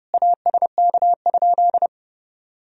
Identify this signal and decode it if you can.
Morse code, heard as a keyed tone, about 32 wpm, 700 Hz. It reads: ASK?
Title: Sentences from Top 200 Words and Abbreviations 30wpm